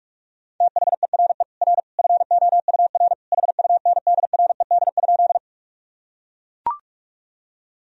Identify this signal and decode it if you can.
Morse code, heard as a keyed tone, about 42 wpm, 700 Hz. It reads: THERE R FOUR HUNDRED?
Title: Sentences from Top 400 Words and Abbreviations 45wpm